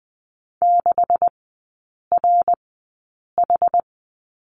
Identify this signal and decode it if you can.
Morse code, heard as a keyed tone, about 20 wpm, spaced as for 8 wpm, 700 Hz. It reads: 6RH
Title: 138 Lesson 23e - Letter f - Three Characters 20wpm